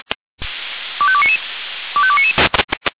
Five-tone selective call (ZVEI), heard as quick runs of five tones, tones of about 70 ms. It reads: 25299 25299